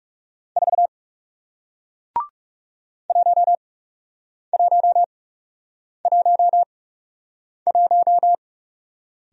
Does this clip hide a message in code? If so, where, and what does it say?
Yes: Morse code from 0.6 to 8.3 s, heard as a keyed tone, about 38 wpm, spaced as for 8 wpm, 700 Hz. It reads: 4 1111